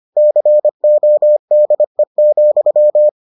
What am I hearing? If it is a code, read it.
Morse code, heard as a keyed tone, about 25 wpm, 600 Hz. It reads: CODE,